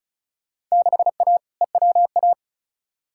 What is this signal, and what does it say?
Morse code, heard as a keyed tone, about 36 wpm, 690 Hz. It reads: 6A EWA
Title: Field Day Exchanges Encore - 35wpm